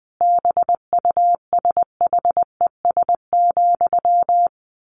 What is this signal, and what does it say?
Morse code, heard as a keyed tone, about 20 wpm, 700 Hz. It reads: BUSHES,
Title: A Sound Of Thunder 20wpm